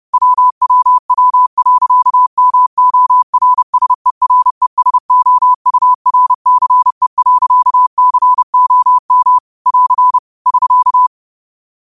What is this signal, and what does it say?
Morse code, heard as a keyed tone, about 30 wpm, 1000 Hz. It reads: WWW.MORSERESOURCE.COM <AR> <SK>
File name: qotdfiles / 30wpm